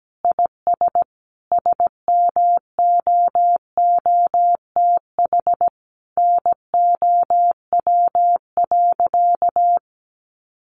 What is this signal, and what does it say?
Morse code, heard as a keyed tone, about 17 wpm, 700 Hz. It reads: IS SMOOTH NOW.
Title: Sentences from Top 1400 Words 17wpm